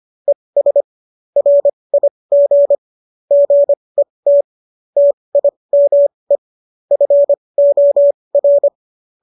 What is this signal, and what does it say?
Morse code, heard as a keyed tone, about 25 wpm, spaced as for 20 wpm, 570 Hz. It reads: ES RIG GET TIME FOR